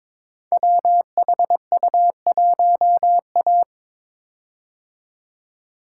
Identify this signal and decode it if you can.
Morse code, heard as a keyed tone, about 22 wpm, 700 Hz. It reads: WHU1A
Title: Code Groups - 5 Letter:Numbers - 22wpm